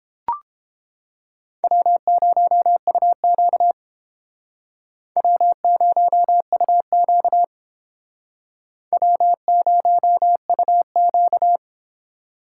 Morse code, heard as a keyed tone, about 30 wpm, 700 Hz. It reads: W0UQ W0UQ W0UQ